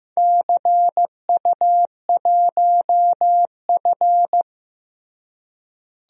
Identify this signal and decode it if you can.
Morse code, heard as a keyed tone, about 15 wpm, 700 Hz. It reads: CU1F
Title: DXCC-100 Call Signs - 4 Characters - Repeated 2x - 15wpm